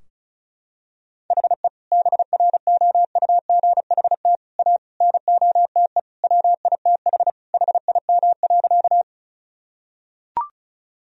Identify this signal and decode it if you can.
Morse code, heard as a keyed tone, about 36 wpm, 700 Hz. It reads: HE BROUGHT A NOTE WITH HIM.
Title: Sentences from Top 1000 Words - Review Part 2 - 35wpm